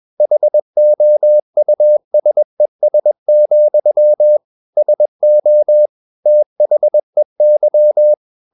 Morse code, heard as a keyed tone, about 21 wpm, 600 Hz. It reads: HOUSES, SO THEY
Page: CW med Gnister